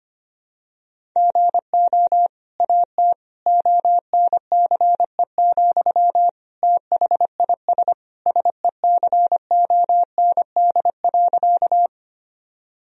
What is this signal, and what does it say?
Morse code, heard as a keyed tone, about 25 wpm, 700 Hz. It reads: GO AT ONCE, THIS SECOND.
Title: Sentences from Top 1000 Words - Review Part 1 - 25wpm